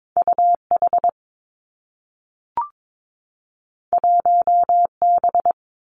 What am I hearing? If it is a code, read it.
Morse code, heard as a keyed tone, about 22 wpm, 700 Hz. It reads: U H 1 B
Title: Two Challenging Letter-Number Combinations 22wpm